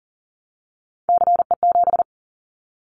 Morse code, heard as a keyed tone, about 40 wpm, 700 Hz. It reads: CE7
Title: Callsign Prefixes for Top 100 DXCC - 40wpm